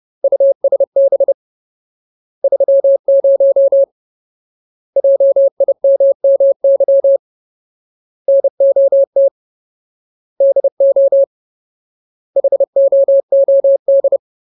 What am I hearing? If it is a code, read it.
Morse code, heard as a keyed tone, about 30 wpm, 550 Hz. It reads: USB 30 JIMMY NOT DO HOOD
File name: Ultimate Word List / 30WPM Ultimate